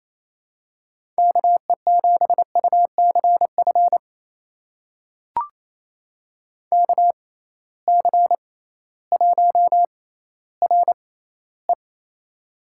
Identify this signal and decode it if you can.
Morse code, heard as a keyed tone, about 28 wpm, spaced as for 15 wpm, 700 Hz. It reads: KE7UCF KC1RE
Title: USA Call Signs - 6x Character Spacing - 28wpm